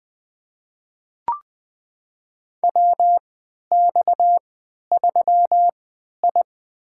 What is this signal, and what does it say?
Morse code, heard as a keyed tone, about 21 wpm, spaced as for 12 wpm, 700 Hz. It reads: WX3I
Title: USA Call Signs - 4 Characters - 3x Character Spacing - 20wpm